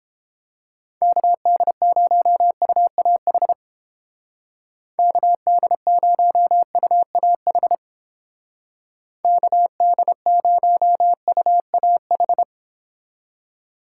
Morse code, heard as a keyed tone, about 29 wpm, 700 Hz. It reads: KD0UAH KD0UAH KD0UAH